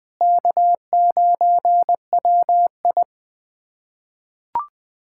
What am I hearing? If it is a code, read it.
Morse code, heard as a keyed tone, about 20 wpm, 700 Hz. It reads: K9WI
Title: K1USN SST QSO Practice - 20wpm